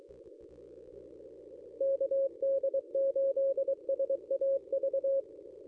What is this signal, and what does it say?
Morse code, heard as a keyed tone, about 23 wpm, 540 Hz. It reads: KD8SAV
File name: contests / ss-cw-k5tr-op-n5rz -1000